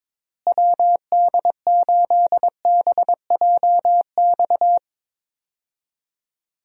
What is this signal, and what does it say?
Morse code, heard as a keyed tone, about 22 wpm, 700 Hz. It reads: WD8BJX
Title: USA Call Signs - 6 Characters - Repeated 3x - 22wpm